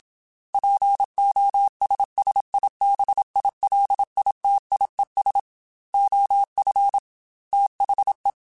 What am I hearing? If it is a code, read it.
Morse code, heard as a keyed tone, about 27 wpm, 800 Hz. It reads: POSSIBILITIES OF THE